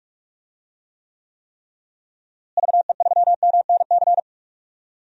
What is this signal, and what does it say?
Morse code, heard as a keyed tone, about 46 wpm, 700 Hz. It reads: VE3MNC
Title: DXCC-100 Call Signs - 6 Characters - Repeated 3x - 45wpm